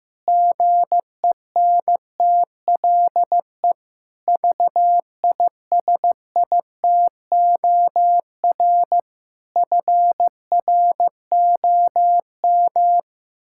Morse code, heard as a keyed tone, about 15 wpm, 700 Hz. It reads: GENTLE VISITOR FROM
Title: Pangram Challenge v2 - 15wpm